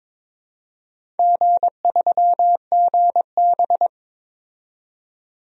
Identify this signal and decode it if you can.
Morse code, heard as a keyed tone, about 22 wpm, 700 Hz. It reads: G3GB
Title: DXCC-100 Call Signs - 4 Characters - 22wpm